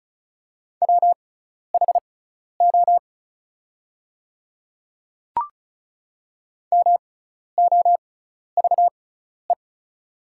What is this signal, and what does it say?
Morse code, heard as a keyed tone, about 36 wpm, spaced as for 12 wpm, 700 Hz. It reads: WHO MOVE
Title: 6x Character Spacing - Top 200 Words - 35wpm